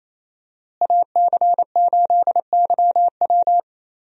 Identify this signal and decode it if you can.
Morse code, heard as a keyed tone, about 28 wpm, 700 Hz. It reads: AC8YW
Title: USA Call Signs - 5 Characters - Repeated 2x - 28wpm